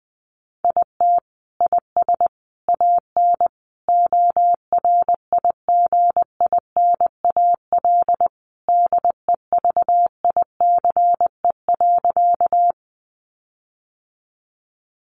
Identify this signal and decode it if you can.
Morse code, heard as a keyed tone, about 20 wpm, 700 Hz. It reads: IT IS AN ORIGINAL DEVICE.
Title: Sentences from Top 1500 Words 20wpm